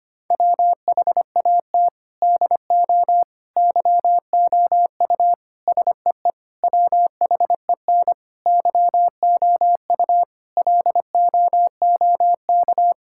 Morse code, heard as a keyed tone, about 25 wpm, 700 Hz. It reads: WHAT DO YOU SEE WHEN YOU LOOK
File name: practice / 25wpm / Sentences with 4-Word Phrases - 25wpm